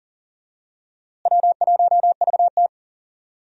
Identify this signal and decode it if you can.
Morse code, heard as a keyed tone, about 40 wpm, 700 Hz. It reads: W1VT
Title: USA Call Signs - 4 Characters - Repeated 3x - 40wpm